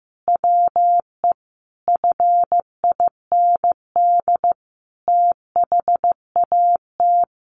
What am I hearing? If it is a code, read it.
Morse code, heard as a keyed tone, about 15 wpm, 700 Hz. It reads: WE FIND THAT
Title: Sentences from Top 1500 Words 15wpm